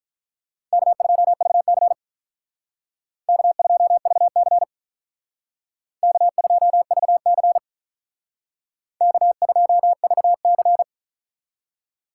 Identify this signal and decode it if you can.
Morse code, heard as a keyed tone, about 44 wpm, 700 Hz. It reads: K2VC K2VC K2VC K2VC